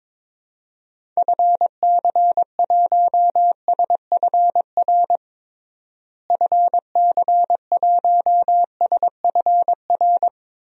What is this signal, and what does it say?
Morse code, heard as a keyed tone, about 22 wpm, 700 Hz. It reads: FC1SFR FC1SFR
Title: DXCC-100 Call Signs - 6 Characters - Repeated 3x - 22wpm